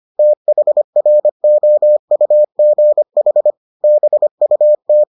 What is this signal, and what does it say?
Morse code, heard as a keyed tone, about 25 wpm, 600 Hz. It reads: THROUGH BUT